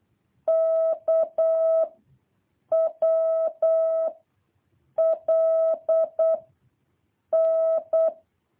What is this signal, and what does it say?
Morse code, heard as a keyed tone, about 8 wpm, 650 Hz. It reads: K W L N
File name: jh8cbh / 2023cw